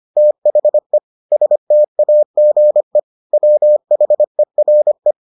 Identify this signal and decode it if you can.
Morse code, heard as a keyed tone, about 25 wpm, 600 Hz. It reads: THE STAGE WHERE